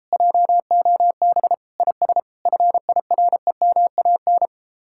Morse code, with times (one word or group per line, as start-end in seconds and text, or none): 0.1-1.5 JOB
1.8-2.2 IS
2.5-4.5 FIREMAN